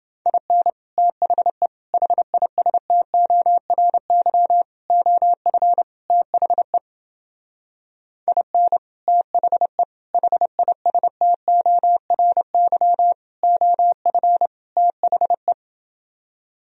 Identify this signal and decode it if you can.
Morse code, heard as a keyed tone, about 28 wpm, 700 Hz. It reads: IN THE HISTORY OF THE IN THE HISTORY OF THE